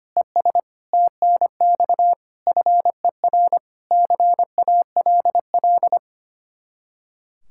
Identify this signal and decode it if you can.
Morse code, heard as a keyed tone, about 30 wpm, 700 Hz. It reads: ES TNX FER CALL
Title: Speed-Racing Common QSO Phrases - 25wpm